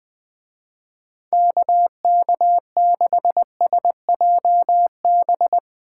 Morse code, heard as a keyed tone, about 20 wpm, 700 Hz. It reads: KK6SJB